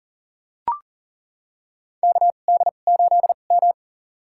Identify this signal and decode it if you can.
Morse code, heard as a keyed tone, about 40 wpm, spaced as for 29 wpm, 700 Hz. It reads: KD8M
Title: USA Call Signs - 4 Characters - 2x Character Spacing - 40wpm